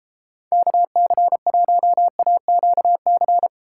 Morse code, heard as a keyed tone, about 31 wpm, 700 Hz. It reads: KC1AQC